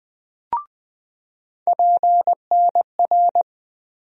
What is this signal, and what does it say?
Morse code, heard as a keyed tone, about 20 wpm, 700 Hz. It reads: PNR
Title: USA Call Sign Postfixes - 3 Letters 20wpm